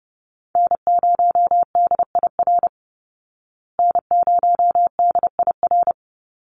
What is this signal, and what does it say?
Morse code, heard as a keyed tone, about 30 wpm, 700 Hz. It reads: N0DIR N0DIR
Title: Call Signs Repeated 2x - 30wpm